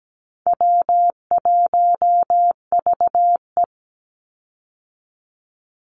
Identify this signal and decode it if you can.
Morse code, heard as a keyed tone, about 17 wpm, 700 Hz. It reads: W1VE
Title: Easy Call Signs Repeated 2x - 17wpm